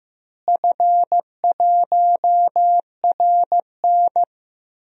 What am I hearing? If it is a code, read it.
Morse code, heard as a keyed tone, about 16 wpm, 700 Hz. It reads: F1RN
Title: DXCC-100 Call Signs - 4 Characters - Repeated 2x - 15wpm